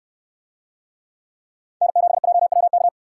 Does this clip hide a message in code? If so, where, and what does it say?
Yes: Morse code from 1.8 to 2.9 s, heard as a keyed tone, about 68 wpm, 700 Hz. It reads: N7JWK